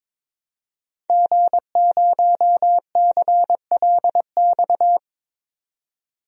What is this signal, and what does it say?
Morse code, heard as a keyed tone, about 21 wpm, 700 Hz. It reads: G0CLX
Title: DXCC-100 Call Signs - 5 Characters - 22wpm